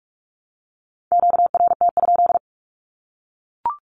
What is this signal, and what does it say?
Morse code, heard as a keyed tone, about 44 wpm, 700 Hz. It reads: QRT?
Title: Q-Codes Repeated 3x 45wpm